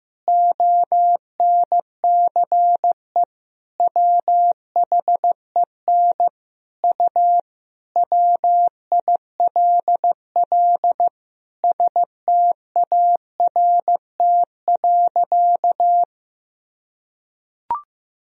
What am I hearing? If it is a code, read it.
Morse code, heard as a keyed tone, about 16 wpm, 700 Hz. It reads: ONCE WHEN U WILL START.
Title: Sentences from Top 300 Words and Abbreviations 15wpm